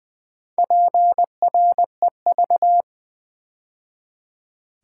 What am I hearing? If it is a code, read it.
Morse code, heard as a keyed tone, about 20 wpm, 700 Hz. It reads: PREV
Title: Build A Word - 2 Letters At A Time - 8-Letter Words - 20wpm